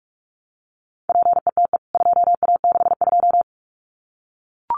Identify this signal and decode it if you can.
Morse code, heard as a keyed tone, about 44 wpm, 700 Hz. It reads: PETE 2A62
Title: CWT Exchange 45wpm